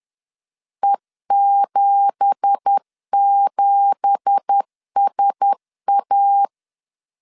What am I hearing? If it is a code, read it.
Morse code, heard as a keyed tone, about 10 wpm, 800 Hz. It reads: E77SA
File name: LEZIONI / 9 2867 10wpm 2733